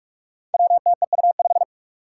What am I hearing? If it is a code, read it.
Morse code, heard as a keyed tone, about 44 wpm, 700 Hz. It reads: WTEU5
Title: Five Challenging Letter-Number Combinations 45wpm